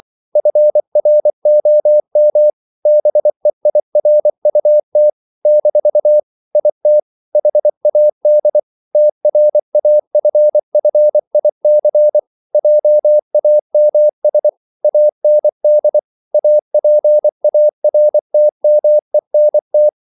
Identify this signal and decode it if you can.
Morse code, heard as a keyed tone, about 24 wpm, 600 Hz. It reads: FROM BEIRUT - IT HAD TRAFFIC JAMS AND APARTMENT